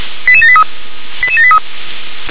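Five-tone selective call (ZVEI), heard as quick runs of five tones, tones of about 70 ms. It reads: 88783 88783
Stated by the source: ZVEI five-tone selective call received on an SDR.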